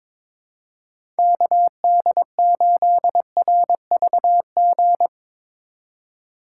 Morse code, heard as a keyed tone, about 22 wpm, 700 Hz. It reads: KD8RVG
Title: USA Call Signs - 6 Characters - 2x Character Spacing - 22wpm